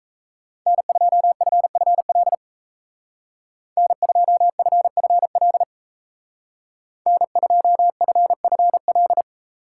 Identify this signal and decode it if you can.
Morse code, heard as a keyed tone, about 36 wpm, 700 Hz. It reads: N2FFL N2FFL N2FFL